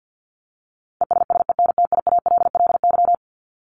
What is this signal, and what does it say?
Morse code, heard as a keyed tone, about 50 wpm, 700 Hz. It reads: ESSENTIALLY
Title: LY Suffix - Top 100 Words with Base Words 50wpm